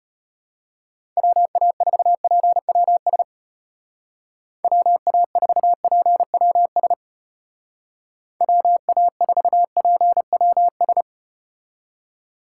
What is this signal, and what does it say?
Morse code, heard as a keyed tone, about 33 wpm, 700 Hz. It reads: WA4PWS WA4PWS WA4PWS